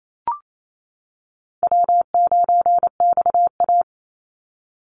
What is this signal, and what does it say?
Morse code, heard as a keyed tone, about 28 wpm, 700 Hz. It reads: W9XA
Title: Easy Call Signs - 28wpm